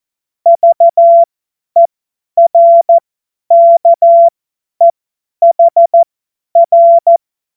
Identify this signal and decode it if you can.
Morse code, heard as a keyed tone, about 14 wpm, spaced as for 10 wpm, 680 Hz. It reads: VERKEHR
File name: Morsekurs / Witze / Witz 25. April 2023 50 F70